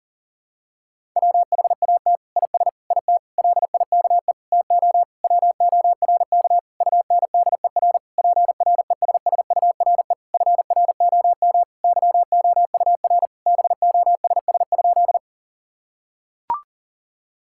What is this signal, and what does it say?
Morse code, heard as a keyed tone, about 40 wpm, 700 Hz. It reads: WHAT IS IT LIKE TO WORK UNDER PRESSURE FROM YOUR BOSS?
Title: Sentences from Top 1000 Words - Review Part 6 - 40wpm